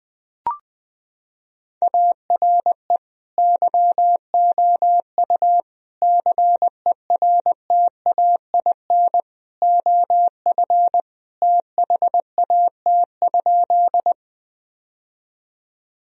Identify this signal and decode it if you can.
Morse code, heard as a keyed tone, about 20 wpm, 700 Hz. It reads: ARE YOU CERTAIN OF THAT?
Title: Sentences from Top 1000 Words - Review Part 2 - 20wpm